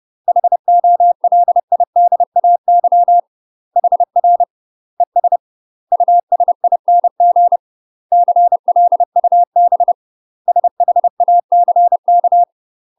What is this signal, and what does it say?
Morse code, heard as a keyed tone, about 30 wpm, 700 Hz. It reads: HOLIDAY HR ES USING CLUB SHACK
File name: UK MASTER QSOs TRAINING / 30WPM / UK MASTER QSOs 0002